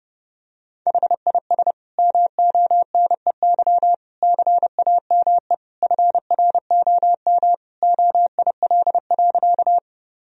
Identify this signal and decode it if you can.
Morse code, heard as a keyed tone, about 32 wpm, 700 Hz. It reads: HIS MONEY CAME FROM OIL.
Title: Sentences from Top 1000 Words - Review Part 4 - 30wpm